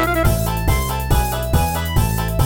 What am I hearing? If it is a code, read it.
There is no code.